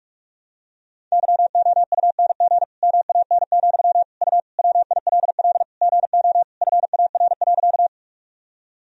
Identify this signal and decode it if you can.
Morse code, heard as a keyed tone, about 46 wpm, 700 Hz. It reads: YOUNG MAN, U WILL GO FAR.
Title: Sentences from Top 400 Words and Abbreviations 45wpm